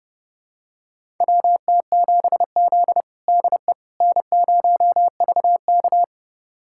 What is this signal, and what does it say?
Morse code, heard as a keyed tone, about 30 wpm, 700 Hz. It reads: WT7Z DE N0VK
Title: USA Callsign DE Callsign - 4 Characters - 30wpm